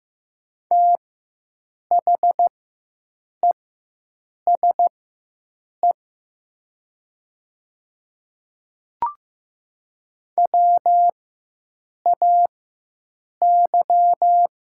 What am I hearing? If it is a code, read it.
Morse code, heard as a keyed tone, about 15 wpm, spaced as for 7 wpm, 700 Hz. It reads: THESE WAY
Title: 4x Character Spacing - Top 100 Words - 15wpm